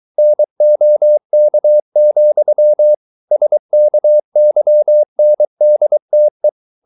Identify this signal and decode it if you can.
Morse code, heard as a keyed tone, about 23 wpm, 600 Hz. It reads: NOK, SKYNDTE
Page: Ferie 23 wpm